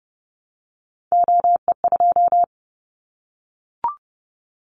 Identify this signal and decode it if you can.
Morse code, heard as a keyed tone, about 30 wpm, 700 Hz. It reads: OE2
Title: Callsign Prefixes for Top 25 DXCC - Repeated 3x - 30wpm